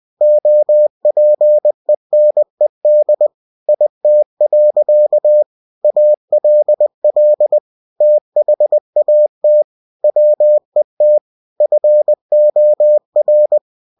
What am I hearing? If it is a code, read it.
Morse code, heard as a keyed tone, about 20 wpm, 600 Hz. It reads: OPENED IT. ALL THAT WET FOR
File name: cw-books / winnie-the-pooh / winnie-20wpm-page-0094